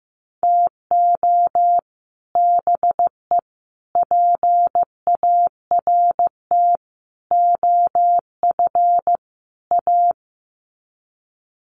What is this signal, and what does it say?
Morse code, heard as a keyed tone, about 15 wpm, 700 Hz. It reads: TO BE PART OF A
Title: Sets of 5 Words Encore2 15wpm